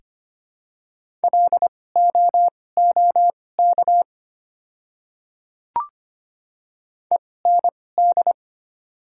Morse code, heard as a keyed tone, about 25 wpm, spaced as for 18 wpm, 690 Hz. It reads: LOOK END
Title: 2x Character Spacing - Top 200 Words - 25wpm